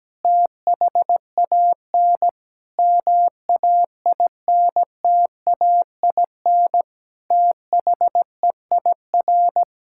Morse code, heard as a keyed tone, about 17 wpm, 700 Hz. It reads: THAN MAINTAIN THEIR
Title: Sentences from Top 1000 Words - Review Part 5 - 17wpm